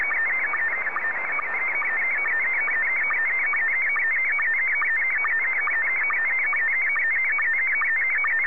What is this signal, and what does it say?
SSTV (slow-scan television), Scottie 1.